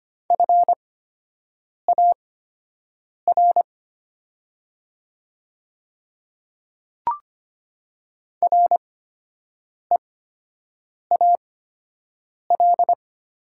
Morse code, heard as a keyed tone, about 25 wpm, spaced as for 7 wpm, 700 Hz. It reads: FAR REAL
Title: 8x Character Spacing - Top 300 Words - 25wpm